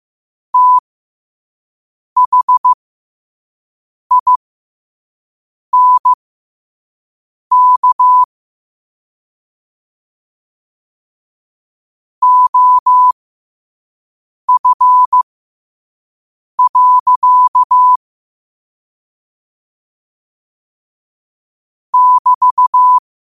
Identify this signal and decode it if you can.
Morse code, heard as a keyed tone, about 15 wpm, spaced as for 5 wpm, 1000 Hz. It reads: THINK OF. <BT>
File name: qotdfiles / 5wpm